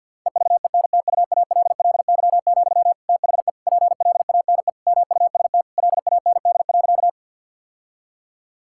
Morse code, heard as a keyed tone, about 50 wpm, 700 Hz. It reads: EVENTUALLY, THE PLANE MUST LAND.
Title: Sentences from Top 1000 Words - Review Part 6 - 50wpm